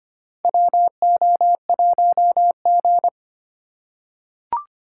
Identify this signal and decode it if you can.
Morse code, heard as a keyed tone, about 25 wpm, 700 Hz. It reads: WO1G
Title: USA Call Signs - 4 Characters - 3x Character Spacing - 25wpm